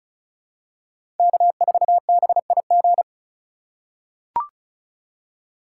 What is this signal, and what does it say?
Morse code, heard as a keyed tone, about 34 wpm, 700 Hz. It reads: K4BIG
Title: USA Call Signs - 5 Characters - 3x Character Spacing - 35wpm